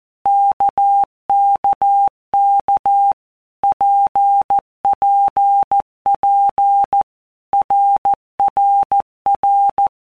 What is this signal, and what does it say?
Morse code, heard as a keyed tone, about 14 wpm, 800 Hz. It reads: KKK PPP RRR
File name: All Groups 1-minute Blocks / Groups@13 / RKPX